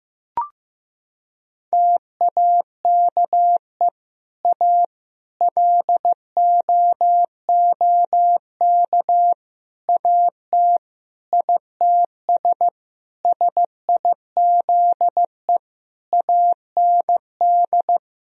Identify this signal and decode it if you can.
Morse code, heard as a keyed tone, about 16 wpm, 700 Hz. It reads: TAKE A LOOK AT ITS SIZE AND
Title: Sentences from Top 1000 Words - Review Part 2 - 15wpm